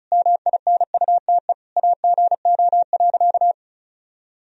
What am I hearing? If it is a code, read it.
Morse code, heard as a keyed tone, about 34 wpm, 700 Hz. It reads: MINUTE AGO.
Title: Sentences from Top 500 Words and Abbreviations 35wpm